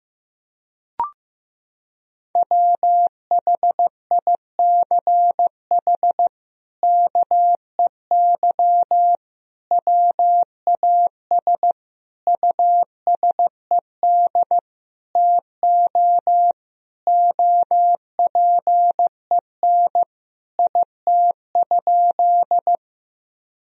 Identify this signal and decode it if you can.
Morse code, heard as a keyed tone, about 15 wpm, 700 Hz. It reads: WHICH KEY WAS USED TO OPEN IT?
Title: Sentences from Top 1000 Words - Review Part 4 - 15wpm